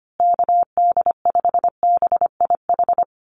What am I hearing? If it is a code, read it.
Morse code, heard as a keyed tone, about 25 wpm, 700 Hz. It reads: KD5BIH
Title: USA Call Signs Repeated 3x 25wpm